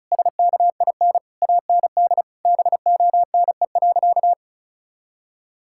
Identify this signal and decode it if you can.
Morse code, heard as a keyed tone, about 36 wpm, 700 Hz. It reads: SKIN AND BONE.
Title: Sentences from Top 1600 Words 35wpm